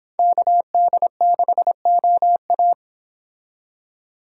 Morse code, heard as a keyed tone, about 29 wpm, 700 Hz. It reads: KD6OA